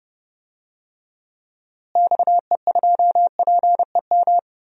Morse code, heard as a keyed tone, about 32 wpm, 700 Hz. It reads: XE2PEM
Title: DXCC-100 Call Signs - 6 Characters - Repeated 3x - 30wpm